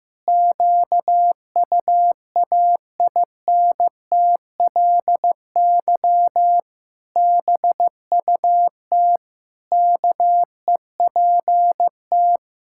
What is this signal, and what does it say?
Morse code, heard as a keyed tone, about 15 wpm, 700 Hz. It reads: QUAINTLY BUT KEPT
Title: Pangram Challenge v2 - 15wpm